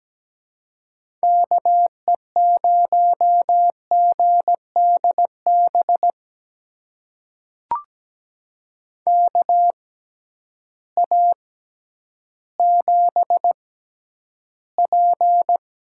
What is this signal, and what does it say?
Morse code, heard as a keyed tone, about 17 wpm, spaced as for 10 wpm, 700 Hz. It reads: KE0GDB KA7P
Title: USA Call Signs - 6 Characters - 6x Character Spacing - 17wpm